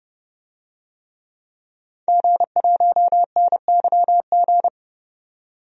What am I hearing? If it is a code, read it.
Morse code, heard as a keyed tone, about 30 wpm, 700 Hz. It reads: G1NYG
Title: DXCC-100 Call Signs - 5 Characters - 30wpm